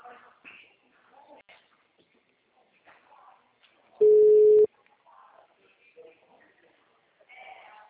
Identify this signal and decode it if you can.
Ringback tone (Europe, 425 Hz), with speech.